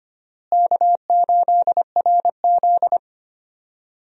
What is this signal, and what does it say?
Morse code, heard as a keyed tone, about 25 wpm, 700 Hz. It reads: K8RZ
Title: USA Call Signs - 4 Characters - Repeated 2x - 25wpm